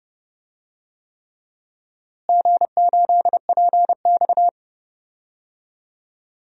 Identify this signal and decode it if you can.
Morse code, heard as a keyed tone, about 31 wpm, 700 Hz. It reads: G8PX
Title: DXCC-100 Call Signs - 4 Characters - Repeated 3x - 30wpm